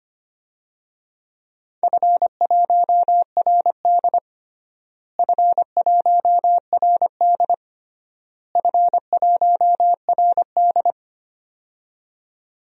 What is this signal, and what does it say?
Morse code, heard as a keyed tone, about 25 wpm, 700 Hz. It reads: F1RD F1RD F1RD
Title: DXCC-100 Call Signs - 4 Characters - Repeated 3x - 25wpm